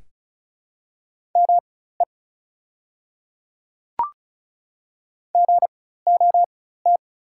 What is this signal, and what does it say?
Morse code, heard as a keyed tone, about 36 wpm, spaced as for 16 wpm, 690 Hz. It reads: ME GOT
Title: 4x Character Spacing - Top 300 Words - 35wpm